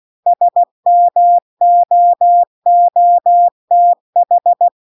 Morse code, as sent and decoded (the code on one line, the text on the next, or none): ... -- --- --- - ....
SMOOTH